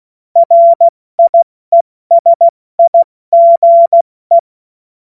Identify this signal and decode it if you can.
Morse code, heard as a keyed tone, about 16 wpm, 680 Hz. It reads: RIESIGE
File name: Morsekurs / Witze / Witz 31. August 2021, 70 F80